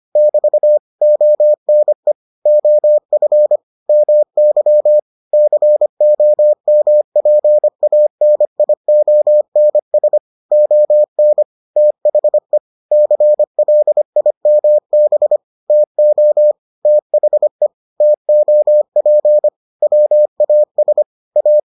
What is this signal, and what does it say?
Morse code, heard as a keyed tone, about 25 wpm, 600 Hz. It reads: <BT> ONE OF MY COMPANIONS ON THE CLIMB TO THE TOP WAS A